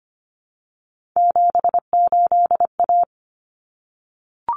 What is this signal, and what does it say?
Morse code, heard as a keyed tone, about 25 wpm, 700 Hz. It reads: 78A
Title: Common Cut Numbers - 3-Digit 25wpm